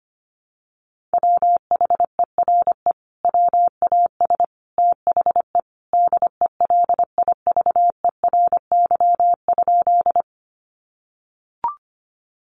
Morse code, heard as a keyed tone, about 25 wpm, 700 Hz. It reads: WHERE WAS THE DELIVERY?
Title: Sentences from Top 1500 Words 25wpm